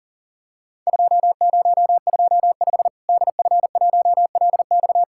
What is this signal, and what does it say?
Morse code, heard as a keyed tone, about 39 wpm, 700 Hz. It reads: 2025 DF1LX
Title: Worked All Europe DX Contest - QTC Exchange - 40wpm